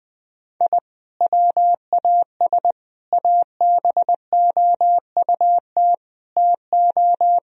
Morse code, heard as a keyed tone, about 20 wpm, 700 Hz. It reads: I WAS ABOUT TO
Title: Sentences from Top 1000 Words - Review Part 1 - 20wpm